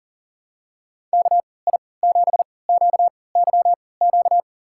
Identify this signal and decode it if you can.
Morse code, heard as a keyed tone, about 40 wpm, spaced as for 23 wpm, 700 Hz. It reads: KI7QYQ
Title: USA Call Signs - 6 Characters - 3x Character Spacing - 40wpm